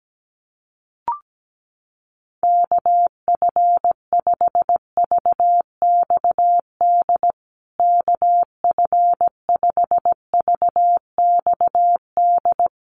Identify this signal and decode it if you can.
Morse code, heard as a keyed tone, about 17 wpm, 700 Hz. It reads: KF5VXD KF5VXD
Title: USA Call Signs Repeated 3x 17wpm